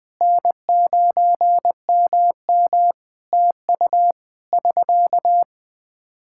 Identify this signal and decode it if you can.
Morse code, heard as a keyed tone, about 20 wpm, 700 Hz. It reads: N9MM TU <SK>
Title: Practice QSO - Exchange 4 - Ending - 20wpm